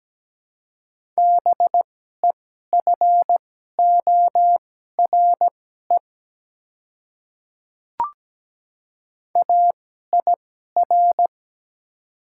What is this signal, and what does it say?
Morse code, heard as a keyed tone, about 17 wpm, spaced as for 12 wpm, 700 Hz. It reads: BEFORE AIR
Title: 2x Character Spacing - Top 200 Words - 17wpm